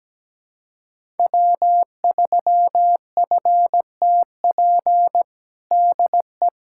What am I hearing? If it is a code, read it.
Morse code, heard as a keyed tone, about 17 wpm, 700 Hz. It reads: W3FTP DE
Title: USA Callsign DE Callsign - 5 Characters - 17wpm